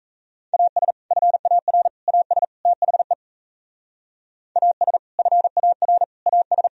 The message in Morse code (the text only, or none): AS FAR AS THE AS FAR AS